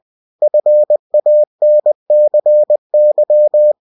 Morse code, heard as a keyed tone, about 20 wpm, 600 Hz. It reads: FANCY